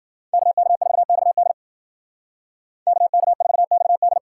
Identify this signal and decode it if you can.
Morse code, heard as a keyed tone, about 52 wpm, 700 Hz. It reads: KK4XD KK4XD